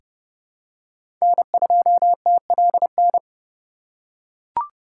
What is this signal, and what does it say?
Morse code, heard as a keyed tone, about 30 wpm, 700 Hz. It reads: N2TLN
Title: USA Call Signs - 5 Characters - 3x Character Spacing - 30wpm